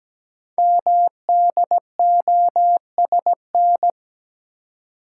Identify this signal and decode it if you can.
Morse code, heard as a keyed tone, about 17 wpm, 700 Hz. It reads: MDOSN
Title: Code Groups - 5 Letter:Numbers - 17wpm